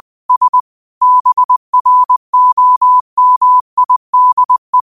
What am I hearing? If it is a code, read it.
Morse code, heard as a keyed tone, about 20 wpm, 1000 Hz. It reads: S BROMIDE
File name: qotdfiles / 20wpm